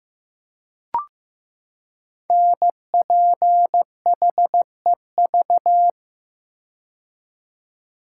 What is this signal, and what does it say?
Morse code, heard as a keyed tone, about 15 wpm, 700 Hz. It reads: NPHEV
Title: Code Groups - 5 Letters - 15wpm